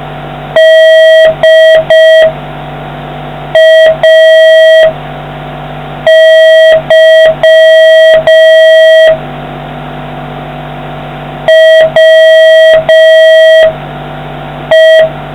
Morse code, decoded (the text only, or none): DAY WE